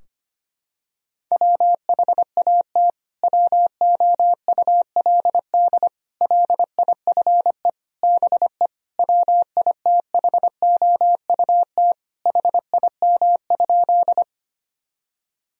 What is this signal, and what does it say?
Morse code, heard as a keyed tone, about 25 wpm, 700 Hz. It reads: WHAT WOULD LIFE BE WITHOUT HIM?
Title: Sentences from Top 1000 Words - Review Part 3 - 25wpm